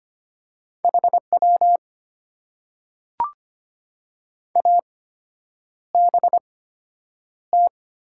Morse code, heard as a keyed tone, about 25 wpm, spaced as for 9 wpm, 700 Hz. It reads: HW ABT
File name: Top 90 QSO Elements - 8x Character Spacing - 25wpm